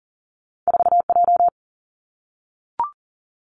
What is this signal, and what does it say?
Morse code, heard as a keyed tone, about 42 wpm, 700 Hz. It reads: 4J
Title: Two Challenging Letter-Number Combinations 40wpm